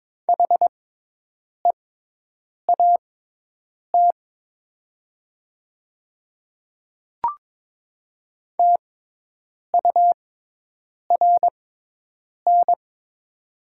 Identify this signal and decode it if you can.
Morse code, heard as a keyed tone, about 22 wpm, spaced as for 8 wpm, 700 Hz. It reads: HEAT TURN
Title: 6x Character Spacing - Top 500 Words - 22wpm